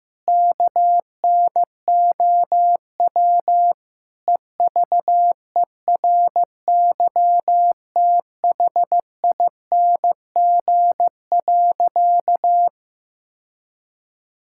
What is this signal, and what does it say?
Morse code, heard as a keyed tone, about 15 wpm, 700 Hz. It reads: KNOW EVERYTHING.
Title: Sentences from Top 1600 Words 15wpm